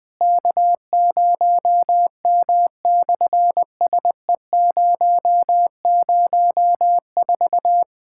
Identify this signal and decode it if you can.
Morse code, heard as a keyed tone, about 20 wpm, 690 Hz. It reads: K0M/SE004
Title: SOTA QSO Practice 20wpm